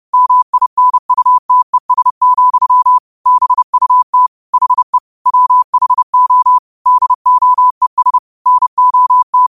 Morse code, heard as a keyed tone, about 30 wpm, 1000 Hz. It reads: MINUTES, BUT HE WHO DOES NOT
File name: qotdfiles / 30wpm